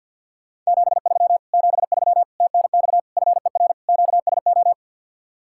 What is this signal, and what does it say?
Morse code, heard as a keyed tone, about 62 wpm, 700 Hz. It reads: 73 73 TNX FER QSO
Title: Speed-Racing Common QSO Phrases - 50wpm